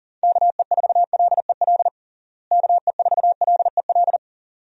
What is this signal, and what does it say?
Morse code, heard as a keyed tone, about 37 wpm, 700 Hz. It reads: KE4LEL KE4LEL
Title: USA Call Signs - 6 Characters - Repeated 3x - 40wpm